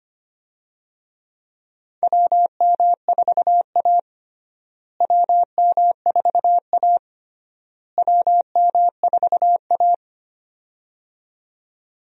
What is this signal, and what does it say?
Morse code, heard as a keyed tone, about 25 wpm, 700 Hz. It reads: WM4A WM4A WM4A
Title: USA Call Signs - 4 Characters - Repeated 3x - 25wpm